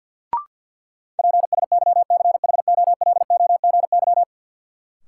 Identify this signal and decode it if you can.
Morse code, heard as a keyed tone, about 53 wpm, 700 Hz. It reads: PSYCHOLOGY
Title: Sentences from Top 1400 Words - New Word Drill - 50wpm